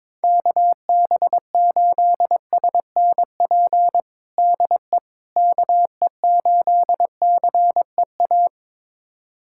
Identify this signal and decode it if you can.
Morse code, heard as a keyed tone, about 22 wpm, 700 Hz. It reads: KB8SNP DE KE8CEA
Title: USA Callsign DE Callsign - 6 Characters - 22wpm